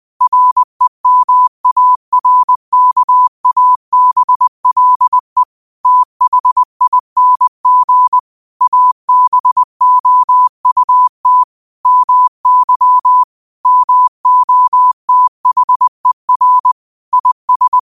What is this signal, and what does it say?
Morse code, heard as a keyed tone, about 20 wpm, 1000 Hz. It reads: REMARKABLE THING ABOUT MY MOTHER IS
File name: qotdfiles / 20wpm